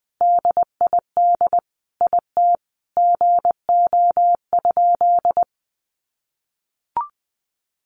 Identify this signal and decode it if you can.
Morse code, heard as a keyed tone, about 20 wpm, 700 Hz. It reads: DID IT GO?
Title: Sentences from Top 200 Words 20wpm